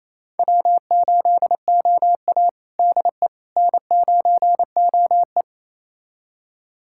Morse code, heard as a keyed tone, about 27 wpm, 690 Hz. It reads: W8OA DE N9OE
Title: USA Callsign DE Callsign - 4 Characters - 28wpm